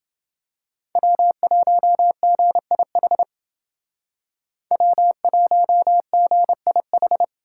Morse code, heard as a keyed tone, about 29 wpm, 700 Hz. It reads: W1GIH W1GIH